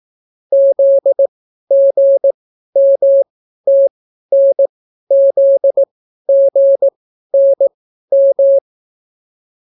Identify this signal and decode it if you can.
Morse code, heard as a keyed tone, about 18 wpm, spaced as for 12 wpm, 550 Hz. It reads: ZGMTNZGNM